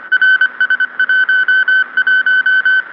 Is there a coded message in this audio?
Yes: Morse code from 0.1 to 2.8 s, heard as a keyed tone, about 25 wpm, 1500 Hz. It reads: RS11